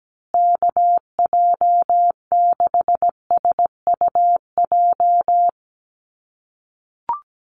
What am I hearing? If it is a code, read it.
Morse code, heard as a keyed tone, about 17 wpm, 700 Hz. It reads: KJ6SUJ
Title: USA Call Signs Repeated 2x 17wpm